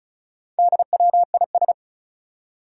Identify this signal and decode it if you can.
Morse code, heard as a keyed tone, about 36 wpm, 700 Hz. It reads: DWIS
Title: Four Challenging Letter-Number Combinations 35wpm